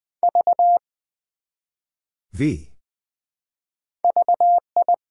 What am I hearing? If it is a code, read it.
Morse code, heard as a keyed tone, about 20 wpm, 700 Hz. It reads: V VI
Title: Build A Word - 1 Letter At A Time - 3 to 5-Letter Words - 20wpm